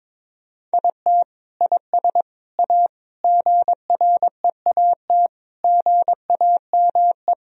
Morse code, heard as a keyed tone, about 22 wpm, 700 Hz. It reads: IT IS A GREAT GAME
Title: Sentences from Top 1000 Words - Review Part 3 - 22wpm